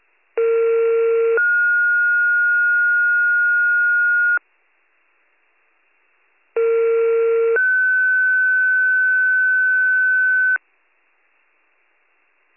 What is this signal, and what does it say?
Two-tone page, heard as a one-second tone, then a three-second tone, about 460 then 1430 Hz, about 460 then 1500 Hz. It is heard from 0.4 to 10.6 s.